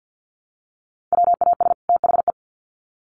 Morse code, heard as a keyed tone, about 50 wpm, 700 Hz. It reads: WAS THE
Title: Sets of 2 Words 50wpm